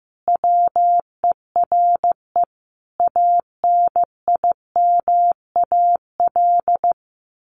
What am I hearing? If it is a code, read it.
Morse code, heard as a keyed tone, about 15 wpm, 700 Hz. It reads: WERE ANIMAL
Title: Sentences from Top 1300 Words 15wpm